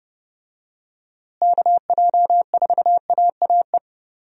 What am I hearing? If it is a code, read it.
Morse code, heard as a keyed tone, about 32 wpm, 700 Hz. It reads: KJ4AAE
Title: USA Call Signs - 6 Characters - Repeated 2x - 30wpm